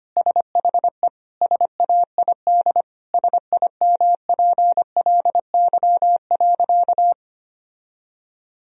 Morse code, heard as a keyed tone, about 25 wpm, 700 Hz. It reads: SHE SAID SIMPLY.
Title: Sentences from Top 1000 Words - Review Part 4 - 25wpm